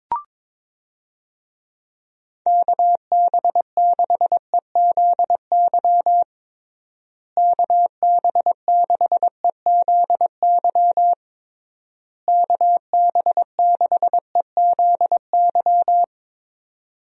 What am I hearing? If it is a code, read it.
Morse code, heard as a keyed tone, about 22 wpm, 700 Hz. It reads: KB6EZY KB6EZY KB6EZY
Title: USA Call Signs - 6 Characters - Repeated 3x - 22wpm